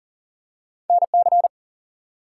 Morse code, heard as a keyed tone, about 39 wpm, 700 Hz. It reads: NC
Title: US States and Canadian Provinces 40wpm